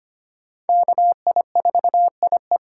Morse code, heard as a keyed tone, about 25 wpm, 700 Hz. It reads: KI4IE
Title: USA Call Signs - 5 Characters - 3x Character Spacing - 25wpm